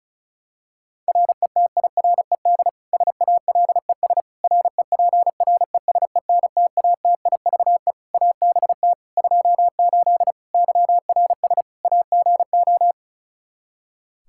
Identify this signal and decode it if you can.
Morse code, heard as a keyed tone, about 34 wpm, 700 Hz. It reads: RETIRED SALES REPRESENTATIVE ABT 28 YRS AGO
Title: Practice QSO - Exchange 3 - More Info - 35wpm